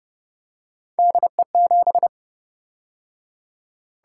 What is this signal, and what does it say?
Morse code, heard as a keyed tone, about 28 wpm, 700 Hz. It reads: DE7